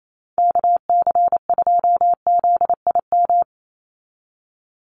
Morse code, heard as a keyed tone, about 28 wpm, 700 Hz. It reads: KC2ZIM
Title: USA Call Signs 28wpm